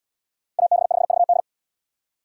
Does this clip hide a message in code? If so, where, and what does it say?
Yes: Morse code from 0.6 to 1.4 s, heard as a keyed tone, about 69 wpm, 700 Hz. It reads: AB5VD